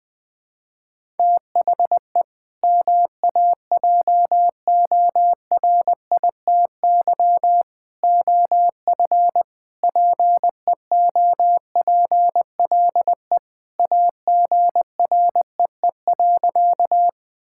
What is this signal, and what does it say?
Morse code, heard as a keyed tone, about 21 wpm, 700 Hz. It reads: THE MAJORITY OF PEOPLE AGREE.
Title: Sentences with 3-Word Phrases - 20wpm